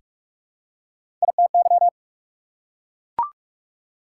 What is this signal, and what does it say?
Morse code, heard as a keyed tone, about 44 wpm, 690 Hz. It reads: ITY
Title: Prefixes and Suffixes 45wpm